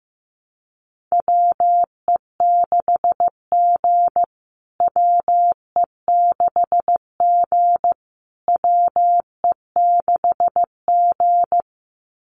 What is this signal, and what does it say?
Morse code, heard as a keyed tone, about 14 wpm, 700 Hz. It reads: WE6G WE6G WE6G
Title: USA Call Signs Repeated 3x 15wpm